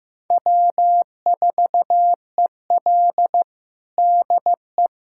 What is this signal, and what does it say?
Morse code, heard as a keyed tone, about 16 wpm, 700 Hz. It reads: W4EL DE
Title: USA Callsign DE Callsign - 4 Characters - 15wpm